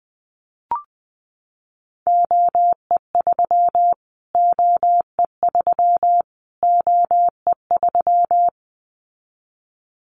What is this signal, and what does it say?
Morse code, heard as a keyed tone, about 20 wpm, 700 Hz. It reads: OE3 OE3 OE3
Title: Callsign Prefixes for Top 25 DXCC - Repeated 3x - 20wpm